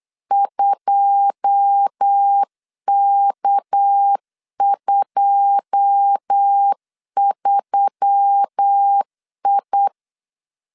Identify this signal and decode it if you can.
Morse code, heard as a keyed tone, about 8 wpm, 800 Hz. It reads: 2K23I
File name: CW / DET / 1 9 1500C S B 8wpm 5624